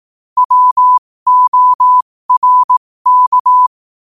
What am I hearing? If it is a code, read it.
Morse code, heard as a keyed tone, about 18 wpm, spaced as for 15 wpm, 1000 Hz. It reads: WORK
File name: qotdfiles / 15wpm